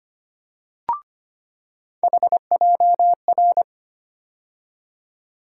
Morse code, heard as a keyed tone, about 25 wpm, 700 Hz. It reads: HJR
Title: USA Call Sign Postfixes - 3 Letters 25wpm